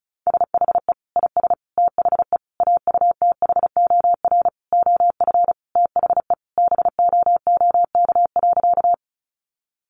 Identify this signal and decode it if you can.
Morse code, heard as a keyed tone, about 34 wpm, 700 Hz. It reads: SHE IS THE AUTHOR OF THE BOOK.
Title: Sentences from Top 1200 Words 35wpm